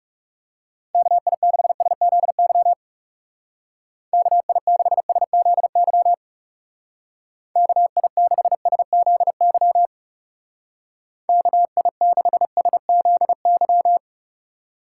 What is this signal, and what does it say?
Morse code, heard as a keyed tone, about 35 wpm, 700 Hz. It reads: KI6SZY KI6SZY KI6SZY KI6SZY
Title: Speed Racing - USA 2x3 Call Signs 30wpm